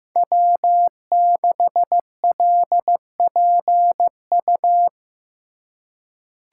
Morse code, heard as a keyed tone, about 15 wpm, 700 Hz. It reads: W6LPU
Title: USA Call Signs - 3x Character Spacing - 15wpm